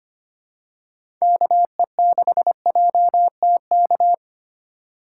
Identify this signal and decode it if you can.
Morse code, heard as a keyed tone, about 25 wpm, 700 Hz. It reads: KE6JTK
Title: USA Call Signs - 6 Characters - 4x Character Spacing - 25wpm